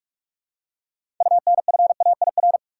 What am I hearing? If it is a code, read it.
Morse code, heard as a keyed tone, about 46 wpm, 700 Hz. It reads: UNFAIR
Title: Six Letter Words - 45wpm